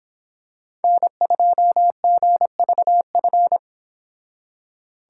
Morse code, heard as a keyed tone, about 26 wpm, 700 Hz. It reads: N2GVF